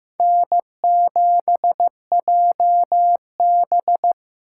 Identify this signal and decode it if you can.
Morse code, heard as a keyed tone, about 15 wpm, 700 Hz. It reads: N7JB
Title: USA Call Signs - 4 Characters - 6x Character Spacing - 15wpm